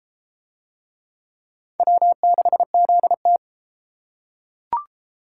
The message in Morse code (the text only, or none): W6ZT